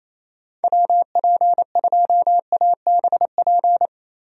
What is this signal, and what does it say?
Morse code, heard as a keyed tone, about 27 wpm, 700 Hz. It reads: WP2ABP
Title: USA Call Signs - 6 Characters - 28wpm